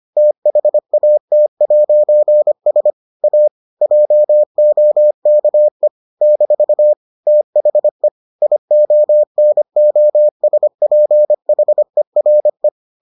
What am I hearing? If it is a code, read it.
Morse code, heard as a keyed tone, about 25 wpm, 600 Hz. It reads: THAT'S A JOKE - THE IONOSPHERE